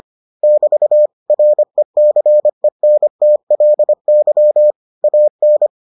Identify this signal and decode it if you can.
Morse code, heard as a keyed tone, about 25 wpm, 600 Hz. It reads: <BT> RECENTLY AN